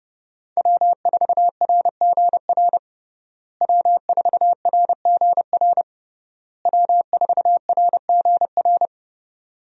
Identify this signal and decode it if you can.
Morse code, heard as a keyed tone, about 32 wpm, 700 Hz. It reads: W4RGR W4RGR W4RGR
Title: USA Call Signs - 5 Characters - Repeated 3x - 30wpm